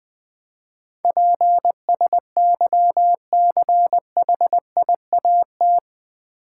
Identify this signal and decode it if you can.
Morse code, heard as a keyed tone, about 20 wpm, 700 Hz. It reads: PSYCHIAT
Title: Build A Word - 4 Letters At A Time - 11-Letter Words - 20wpm